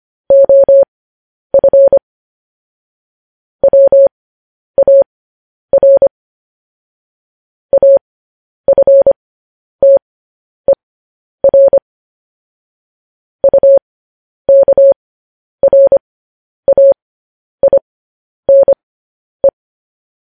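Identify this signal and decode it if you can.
Morse code, heard as a keyed tone, about 25 wpm, spaced as for 10 wpm, 550 Hz. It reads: OF WAR AFTER UKRAINE
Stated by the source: Morse code news headlines practice transmission.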